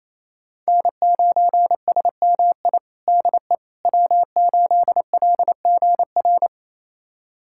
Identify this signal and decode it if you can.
Morse code, heard as a keyed tone, about 28 wpm, 700 Hz. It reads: N9SMI DE W8LGR
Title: USA Callsign DE Callsign - 5 Characters - 28wpm